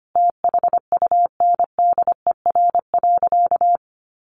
Morse code, heard as a keyed tone, about 25 wpm, 700 Hz. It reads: THUNDER.
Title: A Sound Of Thunder 25wpm